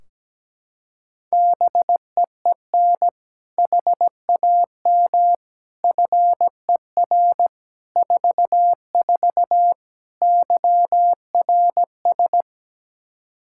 Morse code, heard as a keyed tone, about 17 wpm, 700 Hz. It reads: BEEN HAM FER 44 YRS
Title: Practice QSO - Exchange 3 - More Info - 17wpm